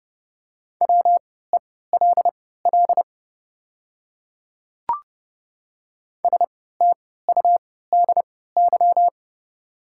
Morse code, heard as a keyed tone, about 30 wpm, spaced as for 17 wpm, 690 Hz. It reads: WELL STUDY
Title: 3x Character Spacing - Top 300 Words - 30wpm